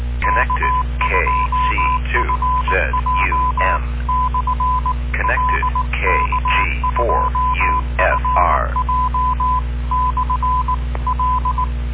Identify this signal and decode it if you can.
Morse code, heard as a keyed tone, about 19 wpm, 1010 Hz, with noise. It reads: K2FJ/L K2FJ/L